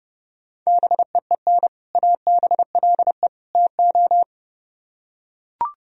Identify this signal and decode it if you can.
Morse code, heard as a keyed tone, about 32 wpm, 700 Hz. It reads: BEEN ABLE TO